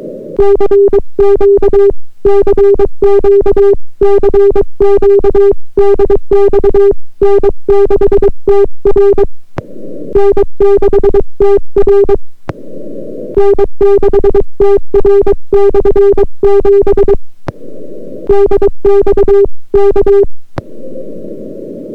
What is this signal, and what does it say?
Morse code, heard as a keyed tone, about 22 wpm, 400 Hz. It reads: CQ CQ CQ DX N6TR N6TR N6TR/7 DX K